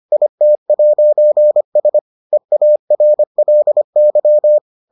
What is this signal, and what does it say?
Morse code, heard as a keyed tone, about 25 wpm, 600 Hz. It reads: IT'S EARLY